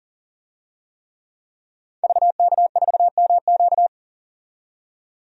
Morse code, heard as a keyed tone, about 40 wpm, 700 Hz. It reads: VK4MQ
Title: DXCC-100 Call Signs - 5 Characters - 40wpm